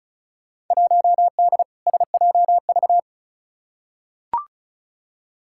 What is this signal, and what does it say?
Morse code, heard as a keyed tone, about 36 wpm, 700 Hz. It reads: 1D SJV
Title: Field Day Exchanges Encore - 35wpm